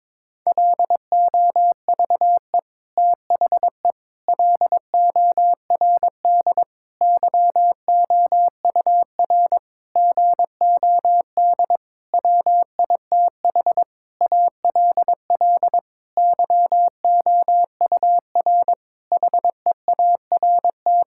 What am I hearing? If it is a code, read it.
Morse code, heard as a keyed tone, about 22 wpm, 700 Hz. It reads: LOVE THE LORD YOUR GOD WITH ALL YOUR HEART
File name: Sentences with 9-Word Phrases - 22wpm